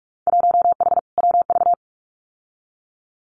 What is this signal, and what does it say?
Morse code, heard as a keyed tone, about 46 wpm, 700 Hz. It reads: 1H WV
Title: Winter Field Day Exchanges 45wpm